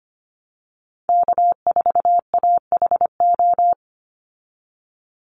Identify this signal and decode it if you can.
Morse code, heard as a keyed tone, about 25 wpm, 700 Hz. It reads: K4AHO
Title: Call Signs Repeated 2x - 25wpm